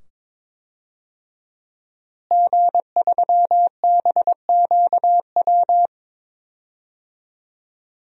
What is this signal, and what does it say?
Morse code, heard as a keyed tone, about 22 wpm, 700 Hz. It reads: G3BQW
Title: DXCC-100 Call Signs - 5 Characters - Repeated 2x - 22wpm